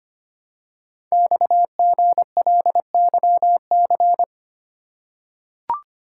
Morse code, heard as a keyed tone, about 25 wpm, 700 Hz. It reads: XGLYC
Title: Code Groups - 5 Letters - 25wpm